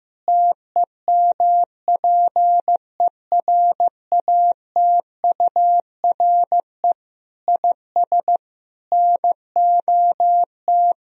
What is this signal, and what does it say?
Morse code, heard as a keyed tone, about 15 wpm, 700 Hz. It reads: TEMPERATURE IS NOT
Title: Sentences from Top 1000 Words - Review Part 4 - 15wpm